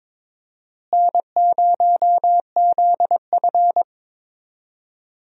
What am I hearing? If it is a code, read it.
Morse code, heard as a keyed tone, about 21 wpm, 700 Hz. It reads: N0ZF
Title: USA Call Signs - 4 Characters - Repeated 3x - 22wpm